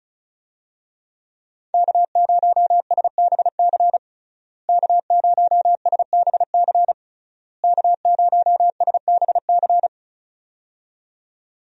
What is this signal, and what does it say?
Morse code, heard as a keyed tone, about 36 wpm, 700 Hz. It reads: K0SBC K0SBC K0SBC
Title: USA Call Signs - 5 Characters - Repeated 3x - 35wpm